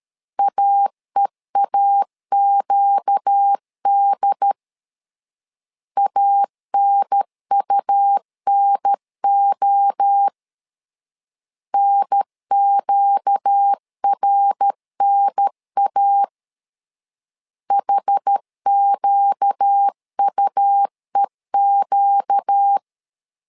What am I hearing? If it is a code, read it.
Morse code, heard as a keyed tone, about 13 wpm, 800 Hz. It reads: AEAQD ANUNO NQRNA HQUEQ
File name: LEZIONI / 4 Q 12wpm 1833